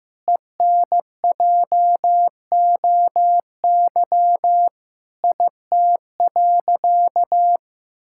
Morse code, heard as a keyed tone, about 15 wpm, 700 Hz. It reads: ENJOY IT.
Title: Sentences from Top 1000 Words - Review Part 5 - 15wpm